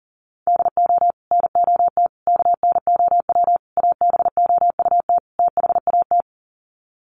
Morse code, heard as a keyed tone, about 40 wpm, 700 Hz. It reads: DO NOT KNOW ABOUT THAT
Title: Sets of 5 Words Encore2 40wpm